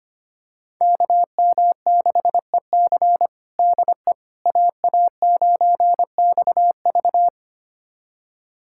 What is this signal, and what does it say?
Morse code, heard as a keyed tone, about 25 wpm, 700 Hz. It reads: KM6EC DE AA9XV
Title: USA Callsign DE Callsign - 5 Characters - 25wpm